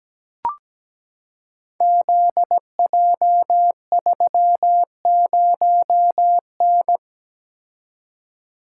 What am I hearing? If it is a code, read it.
Morse code, heard as a keyed tone, about 17 wpm, 700 Hz. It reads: ZJ30N
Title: Code Groups - 5 Letter:Numbers - 17wpm